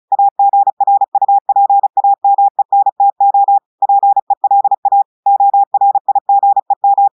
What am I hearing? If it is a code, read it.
Morse code, heard as a keyed tone, about 36 wpm, 800 Hz. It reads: AGRUPAMENTO PELA ORIGEM
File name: audios / portugais / portugais texte en clair 1 vitesse 35 mots minutes